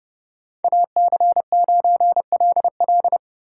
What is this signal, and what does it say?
Morse code, heard as a keyed tone, about 30 wpm, 700 Hz. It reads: AC9LL